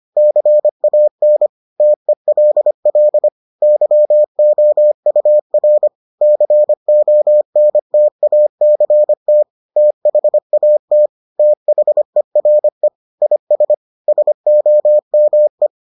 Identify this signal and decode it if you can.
Morse code, heard as a keyed tone, about 25 wpm, 600 Hz. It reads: CAN TELL YOUR CONTACT THAT THERE IS SOME